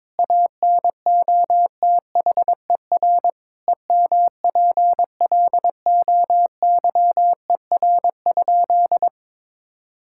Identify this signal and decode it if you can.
Morse code, heard as a keyed tone, about 22 wpm, 700 Hz. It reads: ANOTHER EMPLOYER?
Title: Sentences from Top 1600 Words 22wpm